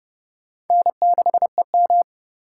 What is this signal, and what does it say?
Morse code, heard as a keyed tone, about 39 wpm, 700 Hz. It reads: N6EM